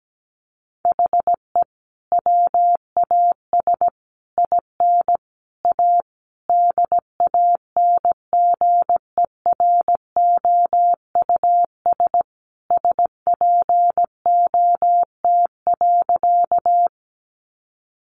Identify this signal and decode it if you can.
Morse code, heard as a keyed tone, about 17 wpm, 700 Hz. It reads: HE WAS IN A DANGEROUS SPOT.
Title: Sentences from Top 1200 Words 17wpm